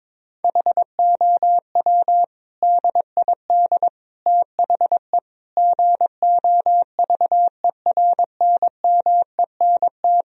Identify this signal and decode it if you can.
Morse code, heard as a keyed tone, about 22 wpm, 700 Hz. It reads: HOW DID THE GOVERNMENT
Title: Sentences from Top 1000 Words - Review Part 4 - 22wpm